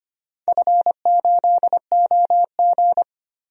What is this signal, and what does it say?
Morse code, heard as a keyed tone, about 25 wpm, 700 Hz. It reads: F8OG
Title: DXCC-100 Call Signs - 4 Characters - 25wpm